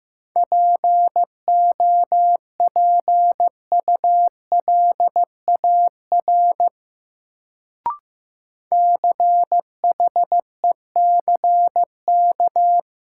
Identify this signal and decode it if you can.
Morse code, heard as a keyed tone, about 15 wpm, 700 Hz. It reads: POPULAR CHECK
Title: Sentences from Top 1000 Words - Review Part 3 - New Word Drill - 15wpm